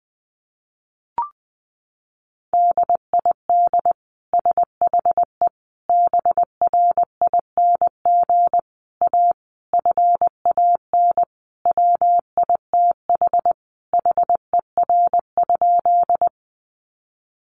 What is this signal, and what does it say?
Morse code, heard as a keyed tone, about 20 wpm, 700 Hz. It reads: DID SHE BRING A FAN WITH HER?
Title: Sentences from Top 1300 Words 20wpm